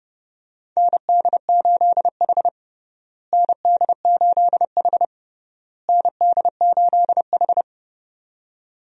Morse code, heard as a keyed tone, about 30 wpm, 700 Hz. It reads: ND8H ND8H ND8H
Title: USA Call Signs - 4 Characters - Repeated 3x - 30wpm